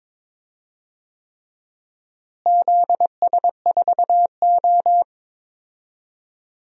Morse code, heard as a keyed tone, about 22 wpm, 700 Hz. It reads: ZS4O
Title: DXCC-100 Call Signs - 4 Characters - Repeated 3x - 22wpm